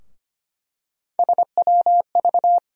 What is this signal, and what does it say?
Morse code, heard as a keyed tone, about 25 wpm, 700 Hz. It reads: SWV
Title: USA Call Sign Postfixes - 3 Letters 25wpm